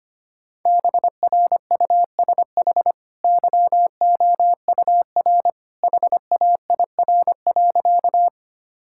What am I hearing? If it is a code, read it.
Morse code, heard as a keyed tone, about 25 wpm, 700 Hz. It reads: BRUSH YOUR HAIR.
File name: Easy Sentences with Hints - 25wpm